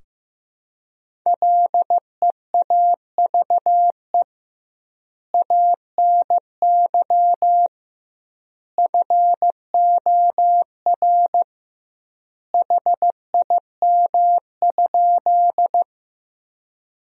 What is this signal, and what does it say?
Morse code, heard as a keyed tone, about 16 wpm, 700 Hz. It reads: LEAVE ANY FOR HIM?
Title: 2x Word-Spacing - Sentences from Top 400 Words - 15wpm